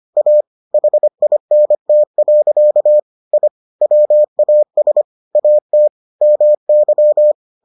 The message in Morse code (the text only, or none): A HINT. I WAS AT MY